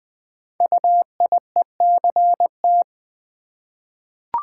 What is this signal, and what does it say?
Morse code, heard as a keyed tone, about 20 wpm, 700 Hz. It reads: UIECT
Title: Code Groups - 5 Letters - 20wpm